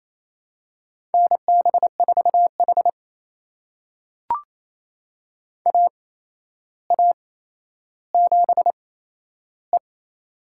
Morse code, heard as a keyed tone, about 28 wpm, spaced as for 12 wpm, 700 Hz. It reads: NB4H AA7E
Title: USA Call Signs - 4 Characters - 8x Character Spacing - 28wpm